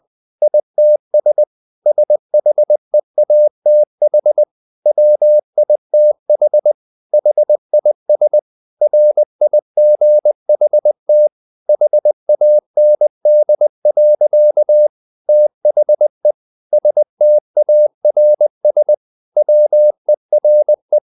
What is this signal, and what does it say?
Morse code, heard as a keyed tone, about 20 wpm, 600 Hz. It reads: ITS SHEATH WITH HIS RIGHT HAND. THE STARS WERE
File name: the old man and the sea part 2 20wpm